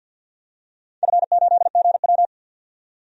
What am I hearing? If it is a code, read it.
Morse code, heard as a keyed tone, about 51 wpm, 700 Hz. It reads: F8GW
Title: DXCC-100 Call Signs - 4 Characters - 50wpm